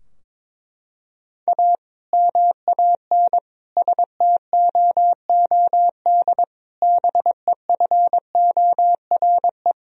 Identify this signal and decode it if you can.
Morse code, heard as a keyed tone, about 22 wpm, 700 Hz. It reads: A MAN STOOD BEFORE
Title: Sentences from Top 1000 Words - Review Part 2 - 22wpm